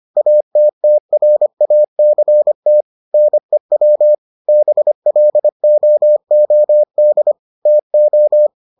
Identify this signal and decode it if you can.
Morse code, heard as a keyed tone, about 25 wpm, 600 Hz. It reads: ATTRACT NEW BLOOD TO